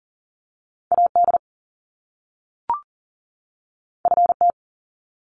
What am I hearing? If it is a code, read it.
Morse code, heard as a keyed tone, about 40 wpm, 700 Hz. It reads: AD FT
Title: Top Two Letter Combinations in Words 40wpm